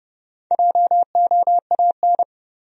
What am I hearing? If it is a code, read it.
Morse code, heard as a keyed tone, about 30 wpm, 700 Hz. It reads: JOAN
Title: USA Common Names - Max 5 Letters - 30wpm